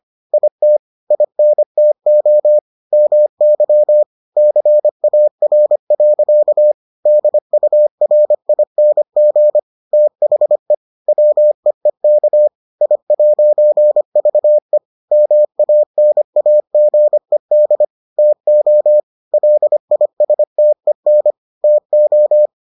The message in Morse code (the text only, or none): IT INTO MY CAR. DURING THE WEEK I'VE MANAGED TO LISTEN TO